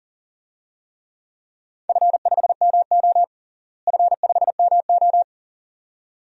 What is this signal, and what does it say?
Morse code, heard as a keyed tone, about 40 wpm, 700 Hz. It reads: F5MO F5MO
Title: DXCC-100 Call Signs - 4 Characters - Repeated 2x - 40wpm